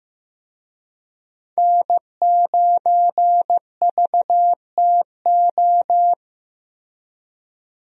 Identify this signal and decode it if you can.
Morse code, heard as a keyed tone, about 15 wpm, 700 Hz. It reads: N9VTO
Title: USA Call Signs - 5 Characters - Repeated 3x - 15wpm